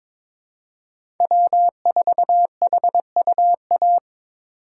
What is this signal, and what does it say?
Morse code, heard as a keyed tone, about 22 wpm, 700 Hz. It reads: W4HUA
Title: USA Call Signs - 5 Characters - 4x Character Spacing - 22wpm